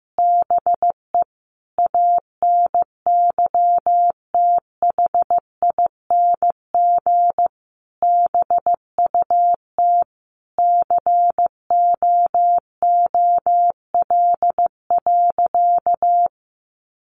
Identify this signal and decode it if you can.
Morse code, heard as a keyed tone, about 15 wpm, 700 Hz. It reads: BE ANYTHING BUT COOL.
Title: Sentences from Top 1200 Words 15wpm